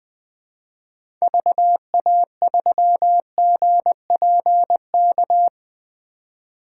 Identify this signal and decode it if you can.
Morse code, heard as a keyed tone, about 21 wpm, 700 Hz. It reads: VA3GPK
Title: 2023 POTA Activator Call Signs - 20wpm